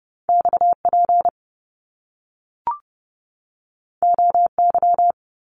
Morse code, heard as a keyed tone, about 30 wpm, 700 Hz. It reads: XP OY
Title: Top Two Letter Combinations in Words 30wpm